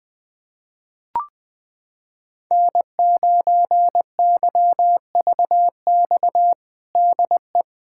Morse code, heard as a keyed tone, about 21 wpm, 700 Hz. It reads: N9YVX DE
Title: USA Callsign DE Callsign - 5 Characters - 20wpm